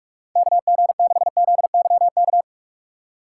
Morse code, heard as a keyed tone, about 42 wpm, 700 Hz. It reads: KG6ZYK
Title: USA Call Signs - 6 Characters - 6x Character Spacing - 45wpm